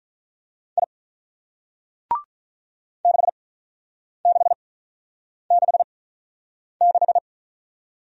Morse code, heard as a keyed tone, about 45 wpm, spaced as for 9 wpm, 700 Hz. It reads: I 6666